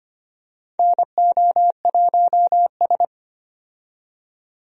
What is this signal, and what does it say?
Morse code, heard as a keyed tone, about 25 wpm, 700 Hz. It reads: NO1S
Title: USA Call Signs - 4 Characters - 25wpm